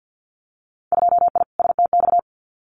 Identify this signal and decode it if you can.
Morse code, heard as a keyed tone, about 50 wpm, 700 Hz. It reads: 2I STX
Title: Winter Field Day Exchanges 50wpm